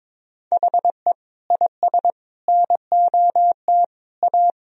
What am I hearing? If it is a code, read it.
Morse code, heard as a keyed tone, about 21 wpm, 700 Hz. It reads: HE IS NOT A
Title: Sentences from Top 1000 Words - Review Part 2 - 22wpm